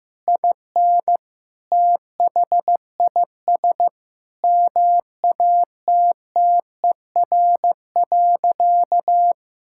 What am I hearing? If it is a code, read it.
Morse code, heard as a keyed tone, about 15 wpm, 700 Hz. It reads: IN THIS MATTER.
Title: Sentences from Top 1600 Words 15wpm